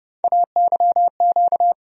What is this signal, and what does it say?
Morse code, heard as a keyed tone, about 30 wpm, 700 Hz. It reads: AYQ
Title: USA Call Sign Postfixes - 3 Letters 30wpm